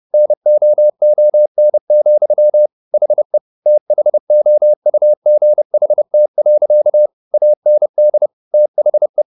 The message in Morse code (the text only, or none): NOON, HE THOUGHT. AND THE